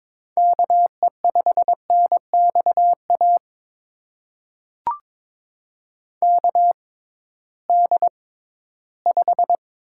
Morse code, heard as a keyed tone, about 22 wpm, spaced as for 14 wpm, 700 Hz. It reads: KE5NXA KD5
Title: USA Call Signs - 6 Characters - 6x Character Spacing - 22wpm